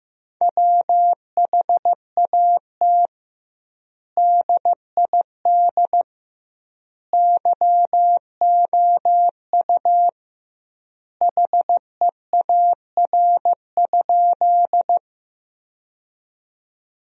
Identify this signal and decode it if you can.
Morse code, heard as a keyed tone, about 15 wpm, 700 Hz. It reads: WHAT DID YOU HEAR?
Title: 2x Word-Spacing - Sentences from Top 300 Words - 15wpm